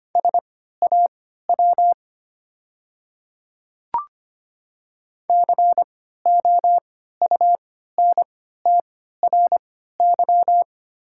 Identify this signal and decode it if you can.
Morse code, heard as a keyed tone, about 25 wpm, spaced as for 14 wpm, 690 Hz. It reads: SAW COUNTRY
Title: 3x Character Spacing - Top 300 Words - 25wpm